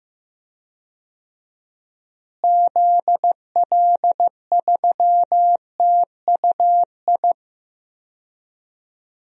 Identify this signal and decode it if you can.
Morse code, heard as a keyed tone, about 15 wpm, 700 Hz. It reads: ZL3TUI
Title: DXCC-100 Call Signs - 6 Characters - Repeated 2x - 15wpm